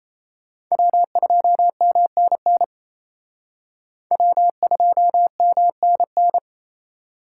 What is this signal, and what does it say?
Morse code, heard as a keyed tone, about 31 wpm, 700 Hz. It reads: W2MNN W2MNN
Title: Speed Racing - USA Call Signs 28wpm